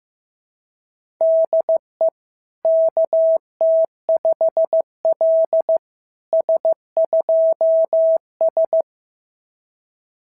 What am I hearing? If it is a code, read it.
Morse code, heard as a keyed tone, about 15 wpm, 650 Hz. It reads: DE KT5L S2S
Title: SOTA QSO Practice 15wpm